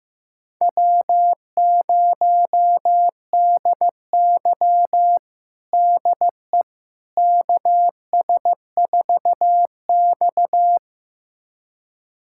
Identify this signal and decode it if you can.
Morse code, heard as a keyed tone, about 15 wpm, 700 Hz. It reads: W0DY DE KS4X
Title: USA Callsign DE Callsign - 4 Characters - 15wpm